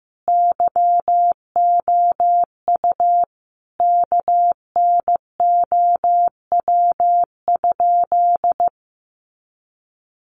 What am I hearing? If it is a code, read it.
Morse code, heard as a keyed tone, about 15 wpm, 700 Hz. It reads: YOU KNOW?
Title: Sentences from Top 100 Words 15wpm